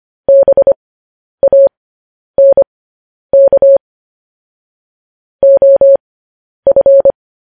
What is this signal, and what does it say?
Morse code, heard as a keyed tone, about 25 wpm, spaced as for 10 wpm, 550 Hz. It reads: BANK OF
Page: news-headlines-in-morse-code-at-10-wpm